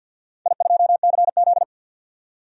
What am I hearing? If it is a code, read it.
Morse code, heard as a keyed tone, about 51 wpm, 700 Hz. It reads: I2CZ
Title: DXCC-100 Call Signs - 4 Characters - 50wpm